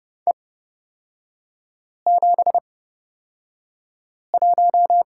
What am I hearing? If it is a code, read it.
Morse code, heard as a keyed tone, about 30 wpm, spaced as for 5 wpm, 700 Hz. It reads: E71
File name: Lesson 36e - BT - Three Characters 30wpm